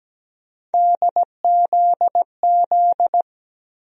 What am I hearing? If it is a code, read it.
Morse code, heard as a keyed tone, about 17 wpm, 700 Hz. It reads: DZZ
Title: USA Call Sign Postfixes - 3 Letters 17wpm